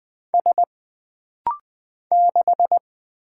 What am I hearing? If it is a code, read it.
Morse code, heard as a keyed tone, about 20 wpm, 700 Hz. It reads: S 6
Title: Single Letter-Number - Rapid-Fire 20wpm